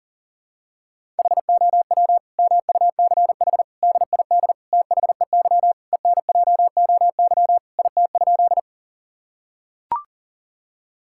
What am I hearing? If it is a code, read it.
Morse code, heard as a keyed tone, about 40 wpm, 700 Hz. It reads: HOW MUCH DID THEY ENJOY IT?
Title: Sentences from Top 1000 Words - Review Part 5 - 40wpm